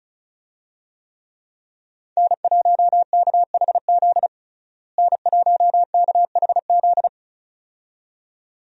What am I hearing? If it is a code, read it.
Morse code, heard as a keyed tone, about 34 wpm, 700 Hz. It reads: N1KHZ N1KHZ
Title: USA Call Signs - 5 Characters - Repeated 2x - 35wpm